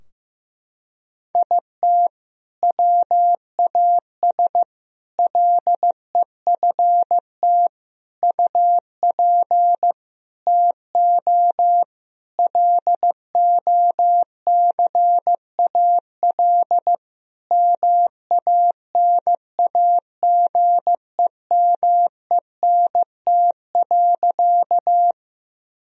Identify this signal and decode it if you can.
Morse code, heard as a keyed tone, about 15 wpm, 700 Hz. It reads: IT WAS LEFT UP TO LOCAL MANAGEMENT.
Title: Sentences from Top 1000 Words - Review Part 4 - 15wpm